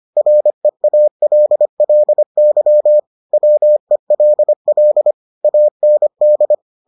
Morse code, heard as a keyed tone, about 25 wpm, 600 Hz. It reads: REALLY WELL AND